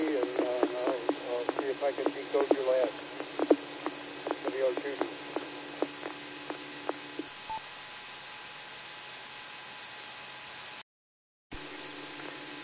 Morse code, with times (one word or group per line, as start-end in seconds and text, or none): none